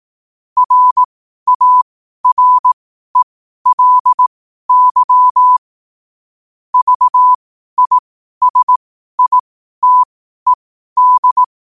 Morse code, heard as a keyed tone, about 18 wpm, spaced as for 12 wpm, 1000 Hz. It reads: RARELY VISITED
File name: qotdfiles / 12wpm